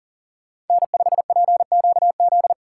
Morse code, heard as a keyed tone, about 49 wpm, 700 Hz. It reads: N5PQZ
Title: Speed Racing - USA 1x3 Call Signs 40wpm